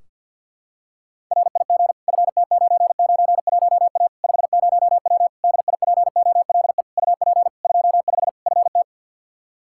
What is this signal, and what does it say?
Morse code, heard as a keyed tone, about 52 wpm, 700 Hz. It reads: RIG FT991A 50W DIPOLE UP 25 FT
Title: Practice QSO - Exchange 2 - Info - 50wpm